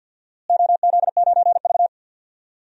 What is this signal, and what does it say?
Morse code, heard as a keyed tone, about 49 wpm, 700 Hz. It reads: OZ9V
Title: DXCC-100 Call Signs - 4 Characters - Repeated 2x - 50wpm